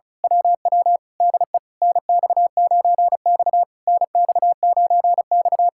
Morse code, heard as a keyed tone, about 33 wpm, 700 Hz. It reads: WW DE NX9X NX9X
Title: Speed-Racing Common QSO Phrases - 35wpm